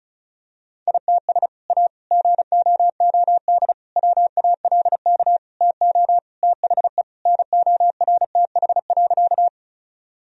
Morse code, heard as a keyed tone, about 34 wpm, 700 Hz. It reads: ITS A GOOD WALK TO THE NORTH.
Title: Sentences from Top 1000 Words - Review Part 1 - 35wpm